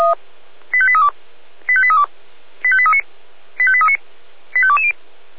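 Five-tone selective call (ZVEI), heard as quick runs of five tones, tones of about 70 ms. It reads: 85832 85832 85839 85839 85220
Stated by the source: ZVEI five-tone call received by SDR, from feuerwehr-orlamuende.org archive.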